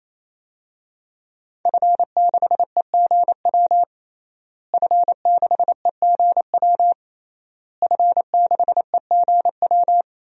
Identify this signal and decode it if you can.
Morse code, heard as a keyed tone, about 28 wpm, 700 Hz. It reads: F6EGW F6EGW F6EGW
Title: DXCC-100 Call Signs - 5 Characters - Repeated 3x - 28wpm